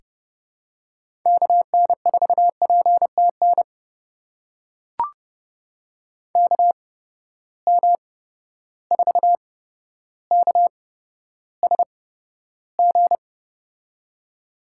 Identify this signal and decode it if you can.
Morse code, heard as a keyed tone, about 30 wpm, spaced as for 13 wpm, 700 Hz. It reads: KN4PTN KM4KSG
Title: USA Call Signs - 8x Character Spacing - 30wpm